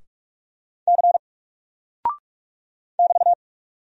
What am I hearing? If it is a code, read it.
Morse code, heard as a keyed tone, about 44 wpm, 700 Hz. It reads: C <BT>
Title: Single Character (Letter:Number:Punctuation:Prosign) Alt 45wpm